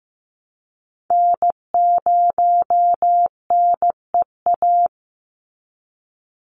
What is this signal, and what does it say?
Morse code, heard as a keyed tone, about 15 wpm, 700 Hz. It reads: N0NEA
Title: USA Call Signs 15wpm